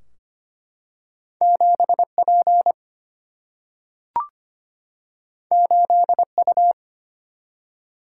Morse code, heard as a keyed tone, about 25 wpm, 700 Hz. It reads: 7 P 8 U
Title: Single-Letter Single-Number 25wpm